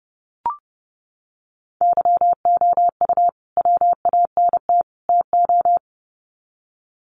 Morse code, heard as a keyed tone, about 30 wpm, 700 Hz. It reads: YOU WANT TO
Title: Sets of 3 Words 30wpm